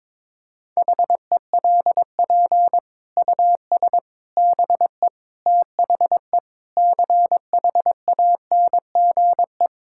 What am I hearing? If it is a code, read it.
Morse code, heard as a keyed tone, about 22 wpm, 700 Hz. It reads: HELP US BE THE CHANGE
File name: Sentences with 9-Word Phrases - 22wpm